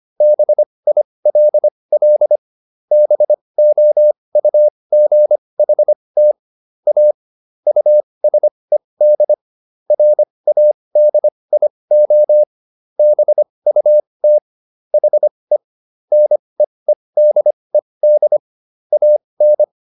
Morse code, heard as a keyed tone, about 25 wpm, spaced as for 20 wpm, 600 Hz. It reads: BILL BOUGHT A USED RADIO BUT HE NEEDED AN